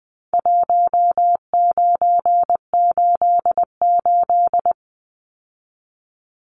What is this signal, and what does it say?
Morse code, heard as a keyed tone, about 20 wpm, 700 Hz. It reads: 1988
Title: Numbers - Contest Style - 4-Digit 20wpm